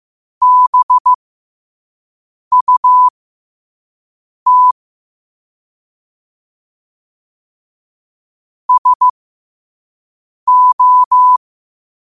Morse code, heard as a keyed tone, about 15 wpm, spaced as for 5 wpm, 1000 Hz. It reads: BUT SO
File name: qotdfiles / 5wpm